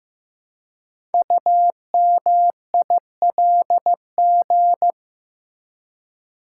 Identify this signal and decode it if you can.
Morse code, heard as a keyed tone, about 15 wpm, 700 Hz. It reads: UMILG
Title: Code Groups - 5 Letters - 15wpm